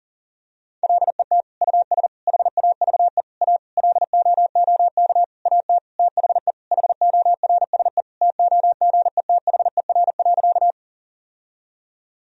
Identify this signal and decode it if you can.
Morse code, heard as a keyed tone, about 40 wpm, 700 Hz. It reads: LET US HAVE A LOOK AT THE HORSE TOGETHER.
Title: Sentences from Top 1000 Words - Review Part 1 - 40wpm